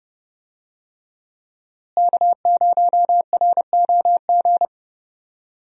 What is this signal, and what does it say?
Morse code, heard as a keyed tone, about 30 wpm, 700 Hz. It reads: K0ROG
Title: USA Call Signs - 5 Characters - Repeated 3x - 30wpm